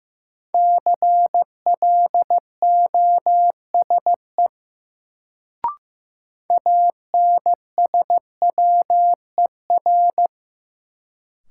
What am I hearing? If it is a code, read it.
Morse code, heard as a keyed tone, about 15 wpm, 700 Hz. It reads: CLOSE ANSWER
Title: Sentences from Top 300 Words - New Word Drill - 15wpm